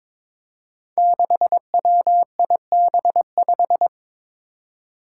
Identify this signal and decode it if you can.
Morse code, heard as a keyed tone, about 22 wpm, 700 Hz. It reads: 6WIB5
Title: Five Challenging Letter-Number Combinations 22wpm